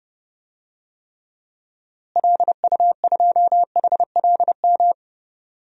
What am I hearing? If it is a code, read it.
Morse code, heard as a keyed tone, about 30 wpm, 700 Hz. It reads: LU2HLM
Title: DXCC-100 Call Signs - 6 Characters - Repeated 3x - 30wpm